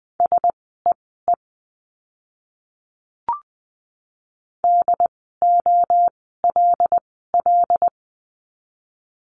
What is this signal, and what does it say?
Morse code, heard as a keyed tone, about 20 wpm, spaced as for 14 wpm, 700 Hz. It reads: SEE DOLL
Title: 075 Lesson 13f - Letter L - Words 20wpm